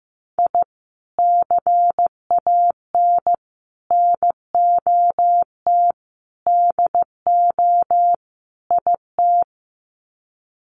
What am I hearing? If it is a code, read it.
Morse code, heard as a keyed tone, about 15 wpm, 700 Hz. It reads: I CAN NOT DO IT
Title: Sets of 5 Words Encore 15wpm